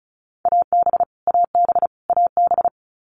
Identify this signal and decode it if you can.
Morse code, heard as a keyed tone, about 34 wpm, 700 Hz. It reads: AB AB AB
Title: Canadian Provinces and Territories - Repeated 3x - 35wpm